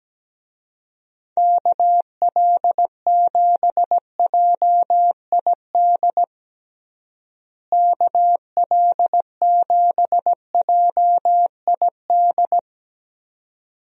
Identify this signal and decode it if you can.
Morse code, heard as a keyed tone, about 17 wpm, 700 Hz. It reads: KL7JID KL7JID
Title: DXCC-100 Call Signs - 6 Characters - Repeated 2x - 17wpm